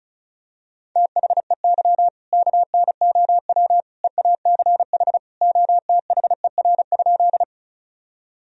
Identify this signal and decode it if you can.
Morse code, heard as a keyed tone, about 36 wpm, 700 Hz. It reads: THEY KNOW EACH OTHER?
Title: Sentences from Top 1000 Words - Review Part 1 - 35wpm